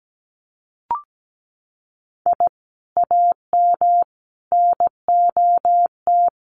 Morse code, heard as a keyed tone, about 17 wpm, 700 Hz. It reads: I AM NOT
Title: Sentences from Top 600 Words 17wpm